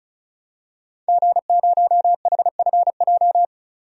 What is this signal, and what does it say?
Morse code, heard as a keyed tone, about 36 wpm, 700 Hz. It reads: G0HFJ
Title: DXCC-100 Call Signs - 5 Characters - 35wpm